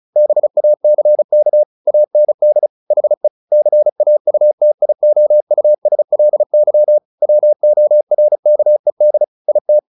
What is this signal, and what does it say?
Morse code, heard as a keyed tone, about 34 wpm, 600 Hz. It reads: BACK AND HE CAUTIOUSLY WORKED IT
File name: the old man and the sea / the old man and the sea 1 35wpm